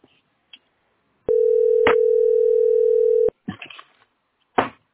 Ringback tone (North America), 440 + 480 Hz.